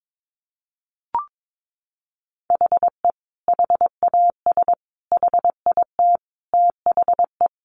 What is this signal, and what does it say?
Morse code, heard as a keyed tone, about 22 wpm, 700 Hz. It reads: HE HAS HIT THE
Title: Sentences from Top 1200 Words 22wpm